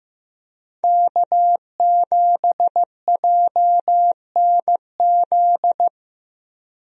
Morse code, heard as a keyed tone, about 16 wpm, 700 Hz. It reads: K7JNZ
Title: USA Call Signs - 5 Characters - Repeated 3x - 15wpm